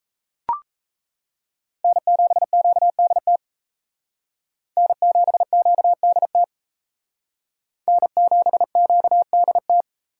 Morse code, heard as a keyed tone, about 36 wpm, 700 Hz. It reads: N7QDT N7QDT N7QDT